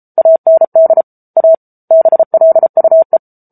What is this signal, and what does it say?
Morse code, heard as a keyed tone, about 34 wpm, 650 Hz. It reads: AND A BLUE
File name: Book Two / 07-Chapter 7 - 33 WPM